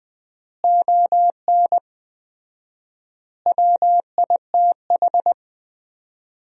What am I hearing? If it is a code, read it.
Morse code, heard as a keyed tone, about 21 wpm, 700 Hz. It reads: ON WITH
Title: 4x Word-Spacing - Sentences from Top 300 Words - 20wpm